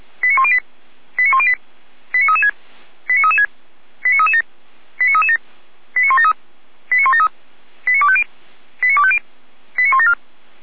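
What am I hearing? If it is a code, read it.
Five-tone selective call (ZVEI), heard as quick runs of five tones, tones of about 70 ms. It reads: 89108 89108 89336 89336 89337 89337 89173 89173 89250 89250 89174
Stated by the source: Five-tone ZVEI selective call received by SDR, Feuerwehr Orlamünde.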